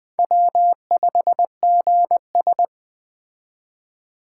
Morse code, heard as a keyed tone, about 20 wpm, 700 Hz. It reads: W5GS
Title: Speed Racing - USA Call Signs 17wpm